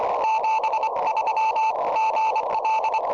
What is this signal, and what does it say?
Morse code, heard as a keyed tone, about 24 wpm, 880 Hz, with radio noise. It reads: 73 GL
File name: tropo / ly1co